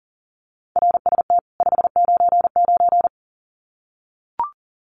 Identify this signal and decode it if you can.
Morse code, heard as a keyed tone, about 40 wpm, 700 Hz. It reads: RST 599
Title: Common QSO Phrases 40wpm